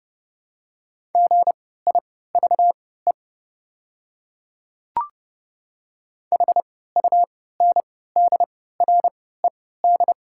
Morse code, heard as a keyed tone, about 30 wpm, spaced as for 17 wpm, 690 Hz. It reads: GIVE HUNDRED
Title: 3x Character Spacing - Top 500 Words - 30wpm